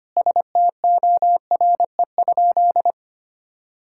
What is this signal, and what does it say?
Morse code, heard as a keyed tone, about 25 wpm, 700 Hz. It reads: STORE?
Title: Sentences from Top 1000 Words - Review Part 5 - 25wpm